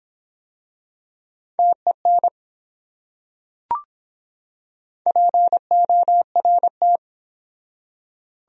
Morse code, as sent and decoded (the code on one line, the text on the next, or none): - . -. / .--. --- .-. -
TEN PORT